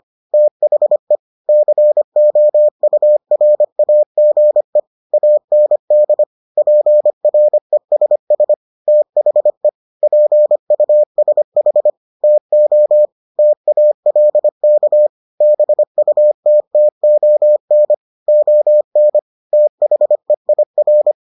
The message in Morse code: THE COURAGE AND PRESS THE PUSH TO TALK BUTTON ON THEIR

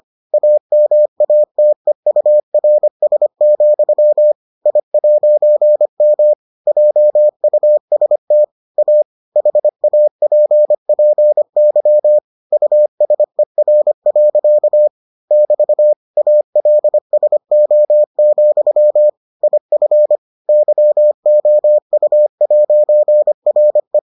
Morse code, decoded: AMATEURS, I'M JUST A HAPPY USER. <BT> ALSO, IF YOU'RE

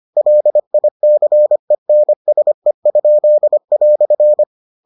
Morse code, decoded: LICENSE?"

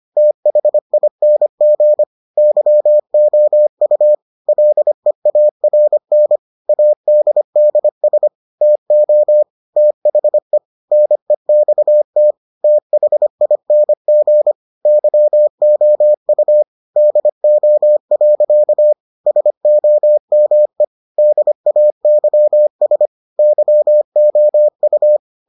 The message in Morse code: THING YOU LEARN ADDS TO THE NEXT THING YOU DO. SOME DAYS YOU

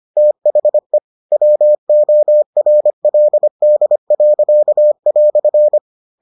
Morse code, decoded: THE WORLD."

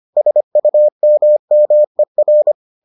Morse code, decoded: SUMMER

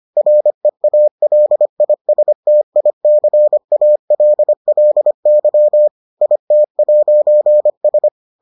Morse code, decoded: REALISTICALLY IT'S

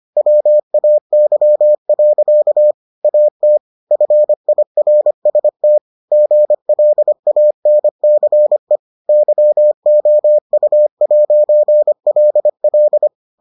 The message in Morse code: WAY. AT FIRST GLANCE YOU'LL

